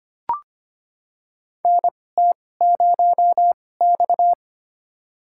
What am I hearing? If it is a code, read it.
Morse code, heard as a keyed tone, about 25 wpm, spaced as for 18 wpm, 700 Hz. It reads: NT0X
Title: USA Call Signs - 4 Characters - 2x Character Spacing - 25wpm